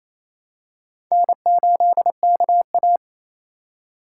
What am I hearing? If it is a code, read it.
Morse code, heard as a keyed tone, about 28 wpm, 700 Hz. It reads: N8KA
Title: USA Call Signs - 4 Characters - 2x Character Spacing - 28wpm